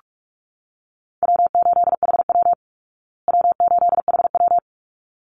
Morse code, heard as a keyed tone, about 44 wpm, 700 Hz. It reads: W8HW W8HW
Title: Easy Call Signs Repeated 2x - 45wpm